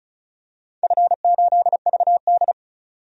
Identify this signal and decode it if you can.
Morse code, heard as a keyed tone, about 36 wpm, 700 Hz. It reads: F8VD
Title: DXCC-100 Call Signs - 4 Characters - 35wpm